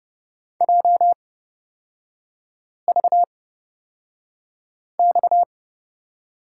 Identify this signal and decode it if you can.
Morse code, heard as a keyed tone, about 30 wpm, spaced as for 5 wpm, 700 Hz. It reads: JVX
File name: Lesson 37e - Letter x - Three Characters 30wpm